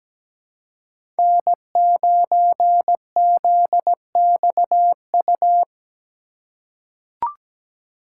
Morse code, heard as a keyed tone, about 17 wpm, 700 Hz. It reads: N9ZXU
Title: USA Call Signs - 3x Character Spacing - 17wpm